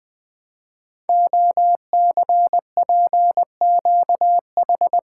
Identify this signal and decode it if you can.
Morse code, heard as a keyed tone, about 20 wpm, 700 Hz. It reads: OCPQH
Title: Code Groups - 5 Letter:Numbers - 20wpm